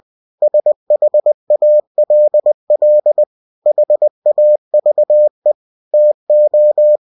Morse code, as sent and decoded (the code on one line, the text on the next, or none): ... .... .- .-.. .-.. / .... .- ...- . / - ---
SHALL HAVE TO